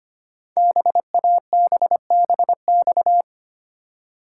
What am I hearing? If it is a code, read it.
Morse code, heard as a keyed tone, about 25 wpm, 700 Hz. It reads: BABBX
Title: Code Groups - 5 Letters - 25wpm